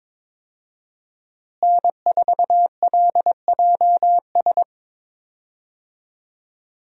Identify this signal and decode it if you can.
Morse code, heard as a keyed tone, about 22 wpm, 700 Hz. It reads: N4LJS
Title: USA Call Signs - 5 Characters - Repeated 3x - 22wpm